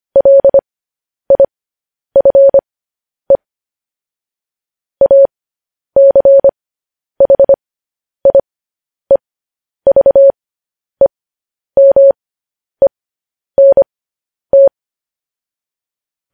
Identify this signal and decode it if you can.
Morse code, heard as a keyed tone, about 25 wpm, spaced as for 10 wpm, 550 Hz. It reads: LIFE ACHIEVEMENT
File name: podcast / News headlines-1x-25x10